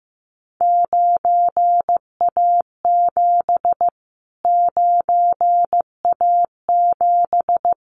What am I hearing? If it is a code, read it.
Morse code, heard as a keyed tone, about 15 wpm, 700 Hz. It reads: 9A7 9A7
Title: Callsign Prefixes for Top 25 DXCC - Repeated 3x - 15wpm